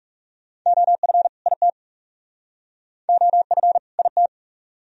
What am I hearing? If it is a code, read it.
Morse code, heard as a keyed tone, about 37 wpm, 700 Hz. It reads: OF IT OF IT
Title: Speed-Racing - Sets of 2 Words 30wpm